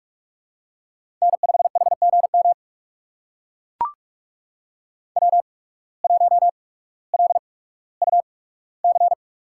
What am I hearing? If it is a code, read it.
Morse code, heard as a keyed tone, about 44 wpm, spaced as for 19 wpm, 700 Hz. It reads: N5HGM W1LUC
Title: USA Call Signs - 5 Characters - 8x Character Spacing - 45wpm